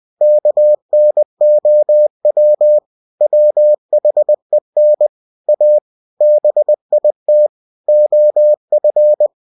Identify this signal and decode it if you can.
Morse code, heard as a keyed tone, about 20 wpm, 600 Hz. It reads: KNOW WHEN A BIT OF
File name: cw-books / winnie-the-pooh / winnie-20wpm-page-0109